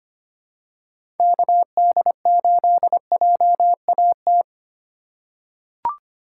Morse code, heard as a keyed tone, about 25 wpm, 700 Hz. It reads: KD8JAT
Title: USA Call Signs - 6x Character Spacing - 25wpm